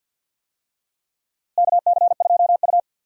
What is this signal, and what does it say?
Morse code, heard as a keyed tone, about 47 wpm, 700 Hz. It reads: KC2U
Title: USA Call Signs - 4 Characters - Repeated 2x - 50wpm